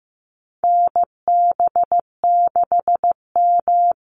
Morse code, heard as a keyed tone, about 15 wpm, 700 Hz. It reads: NB6M
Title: Call Signs Repeated 3x - 15wpm